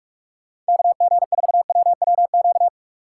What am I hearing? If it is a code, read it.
Morse code, heard as a keyed tone, about 46 wpm, 700 Hz. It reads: KG4WWQ
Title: USA Call Signs - 6 Characters - 2x Character Spacing - 45wpm